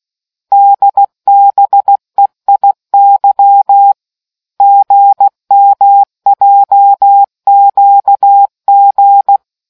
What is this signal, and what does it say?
Morse code, heard as a keyed tone, about 16 wpm, 800 Hz. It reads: DBEIY GMJQG